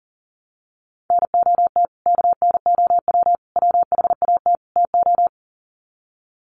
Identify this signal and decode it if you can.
Morse code, heard as a keyed tone, about 40 wpm, 700 Hz. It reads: NOT KNOW WHAT TO
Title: Sets of 4 Words 40wpm